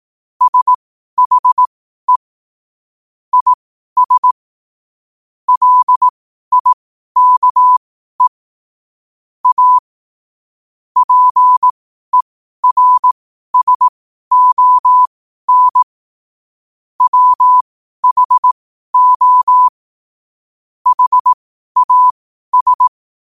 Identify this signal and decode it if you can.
Morse code, heard as a keyed tone, about 18 wpm, spaced as for 12 wpm, 1000 Hz. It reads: SHE IS LIKE A PERSON WHO HAS
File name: 12wpm